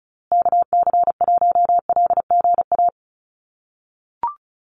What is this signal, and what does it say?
Morse code, heard as a keyed tone, about 34 wpm, 700 Hz. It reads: KC1LGA
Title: USA Call Signs Repeated 3x 35wpm